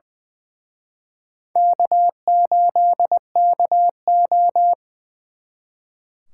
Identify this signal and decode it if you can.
Morse code, heard as a keyed tone, about 24 wpm, 700 Hz. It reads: K8KO
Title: Speed Racing - USA 1x2 Call Signs 20wpm